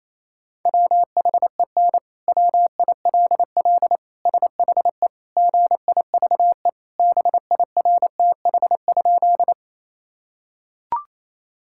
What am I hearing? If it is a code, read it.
Morse code, heard as a keyed tone, about 28 wpm, 700 Hz. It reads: WHEN WILL SHE GIVE BIRTH?
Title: Sentences from Top 1000 Words - Review Part 5 - 28wpm